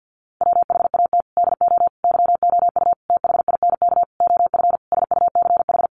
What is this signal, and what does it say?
Morse code, heard as a keyed tone, about 50 wpm, 700 Hz. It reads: WHAT DO YOU THINK OF SUCH
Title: Sentences from Top 1400 Words 50wpm